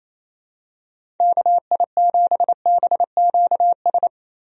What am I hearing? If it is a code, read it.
Morse code, heard as a keyed tone, about 28 wpm, 700 Hz. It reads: KI7BQS
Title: USA Call Signs - 6 Characters - Repeated 3x - 28wpm